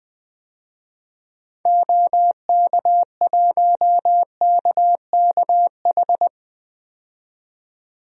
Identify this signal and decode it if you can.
Morse code, heard as a keyed tone, about 21 wpm, 700 Hz. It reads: OK1KKH
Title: DXCC-100 Call Signs - 6 Characters - 20wpm